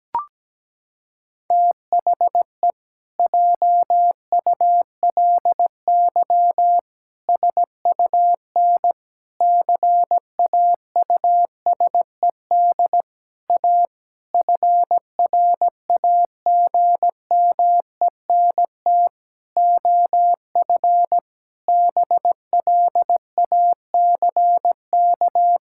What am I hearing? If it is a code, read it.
Morse code, heard as a keyed tone, about 17 wpm, 700 Hz. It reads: THE JULY SUN CAUSED A FRAGMENT OF BLACK
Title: Pangram Challenge v2 - 17wpm